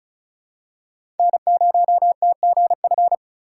Morse code, heard as a keyed tone, about 36 wpm, 700 Hz. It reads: N0TGF
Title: USA Call Signs - 5 Characters - 2x Character Spacing - 35wpm